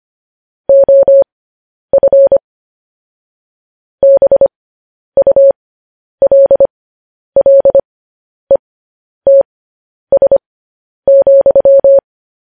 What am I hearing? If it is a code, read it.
Morse code, heard as a keyed tone, about 25 wpm, spaced as for 10 wpm, 550 Hz. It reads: OF BULLETS,